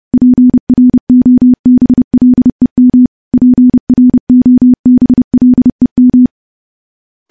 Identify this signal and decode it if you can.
Morse code, heard as a keyed tone, about 30 wpm, 250 Hz. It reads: PROBLEM PROBLEM
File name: morse / morse-training-30WPM